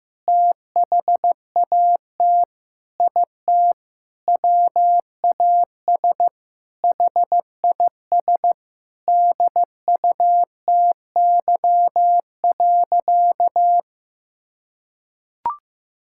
Morse code, heard as a keyed tone, about 15 wpm, 700 Hz. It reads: THAT IT WAS HIS DUTY.
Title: Sentences from Top 1600 Words 15wpm